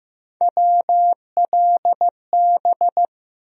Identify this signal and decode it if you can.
Morse code, heard as a keyed tone, about 15 wpm, 700 Hz. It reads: WLB
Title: USA Call Sign Postfixes - 3 Letters 15wpm